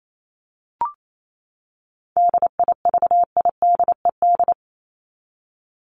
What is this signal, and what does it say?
Morse code, heard as a keyed tone, about 28 wpm, 700 Hz. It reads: DIVIDED
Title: ED Suffix - Top 100 Words with Base Words 28wpm